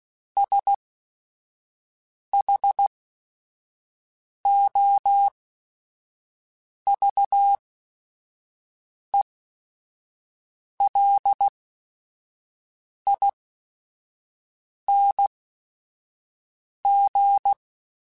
Morse code, heard as a keyed tone, about 16 wpm, spaced as for 5 wpm, 800 Hz. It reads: SHOVELING